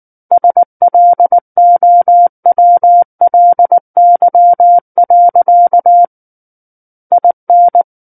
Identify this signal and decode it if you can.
Morse code, heard as a keyed tone, about 19 wpm, 700 Hz. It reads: SLOWLY. IN
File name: Book One / 10-Chapter 10 - 19 WPM